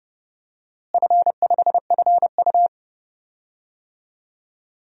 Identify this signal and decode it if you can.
Morse code, heard as a keyed tone, about 30 wpm, 700 Hz. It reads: F5FU
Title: DXCC-100 Call Signs - 4 Characters - Repeated 2x - 30wpm